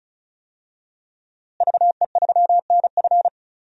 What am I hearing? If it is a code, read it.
Morse code, heard as a keyed tone, about 36 wpm, 700 Hz. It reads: VE3NF
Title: DXCC-100 Call Signs - 5 Characters - Repeated 2x - 35wpm